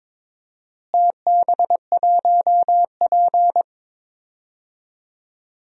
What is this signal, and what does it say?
Morse code, heard as a keyed tone, about 22 wpm, 700 Hz. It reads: TB1P
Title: DXCC-100 Call Signs - 4 Characters - Repeated 3x - 22wpm